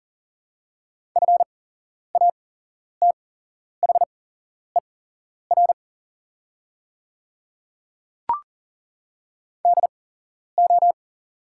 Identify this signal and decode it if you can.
Morse code, heard as a keyed tone, about 40 wpm, spaced as for 11 wpm, 700 Hz. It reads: FATHER DO
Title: 8x Character Spacing - Top 300 Words - 40wpm